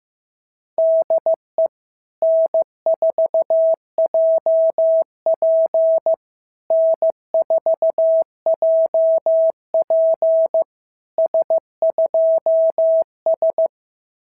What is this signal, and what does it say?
Morse code, heard as a keyed tone, about 15 wpm, 650 Hz. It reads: DE N4JP N4JP S2S
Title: SOTA QSO Practice 15wpm